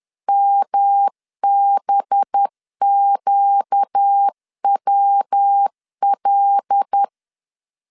Morse code, heard as a keyed tone, about 10 wpm, 800 Hz. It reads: MBQWL
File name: CW / DET / 1 6 1500C S B 10wpm 4137